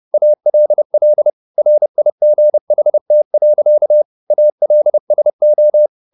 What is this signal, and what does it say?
Morse code, heard as a keyed tone, about 30 wpm, 600 Hz. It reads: ALL RIGHT. ALSO